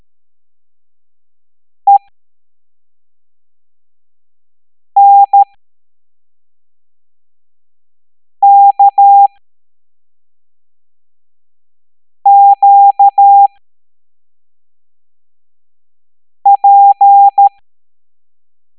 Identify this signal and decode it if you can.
Morse code, heard as a keyed tone, about 13 wpm, spaced as for 3 wpm, 800 Hz. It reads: ENKQP
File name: hamdata / L07-EX02-HQP